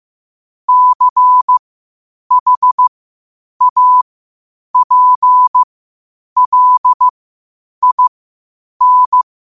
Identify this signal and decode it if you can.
Morse code, heard as a keyed tone, about 15 wpm, spaced as for 8 wpm, 1000 Hz. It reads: CHAPLIN
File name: qotdfiles / 8wpm